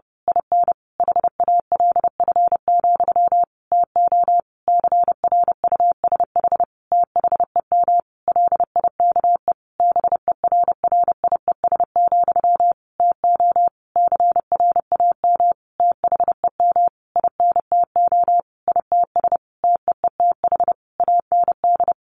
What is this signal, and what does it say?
Morse code, heard as a keyed tone, about 30 wpm, 700 Hz. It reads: IN HALF, TO CRUSH THEM LIKE BERRIES, TO CRAM THEM INTO ITS TEETH AND
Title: A Sound Of Thunder 30wpm